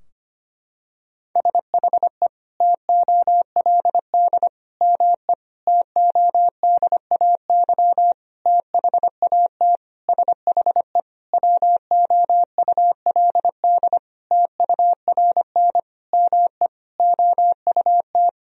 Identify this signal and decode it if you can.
Morse code, heard as a keyed tone, about 25 wpm, 700 Hz. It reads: SHE TOLD ME TODAY THAT SHE WOULD TURN ME OUT
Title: Sentences from Top 1000 Words - Review Part 4 - 25wpm